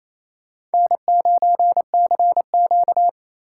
Morse code, heard as a keyed tone, about 37 wpm, 700 Hz. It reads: N9CQ